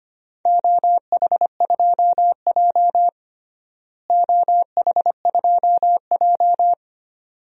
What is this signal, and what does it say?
Morse code, heard as a keyed tone, about 25 wpm, 700 Hz. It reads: OH2J OH2J
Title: DXCC-100 Call Signs - 4 Characters - Repeated 2x - 25wpm